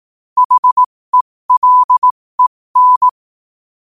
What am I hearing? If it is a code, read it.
Morse code, heard as a keyed tone, about 18 wpm, spaced as for 15 wpm, 1000 Hz. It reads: HELEN
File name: qotdfiles / 15wpm